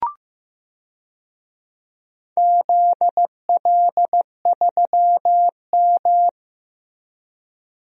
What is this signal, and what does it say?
Morse code, heard as a keyed tone, about 15 wpm, 700 Hz. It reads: ZL3M
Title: DXCC-100 Call Signs - 4 Characters - Repeated 2x - 15wpm